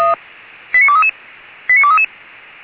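Five-tone selective call (ZVEI), heard as quick runs of five tones, tones of about 70 ms. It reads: 89233 89233
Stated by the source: ZVEI five-tone selective call received on an SDR.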